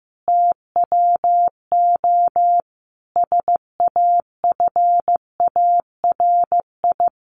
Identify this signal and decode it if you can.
Morse code, heard as a keyed tone, about 15 wpm, 700 Hz. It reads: TWO SAFARI
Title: A Sound Of Thunder 15wpm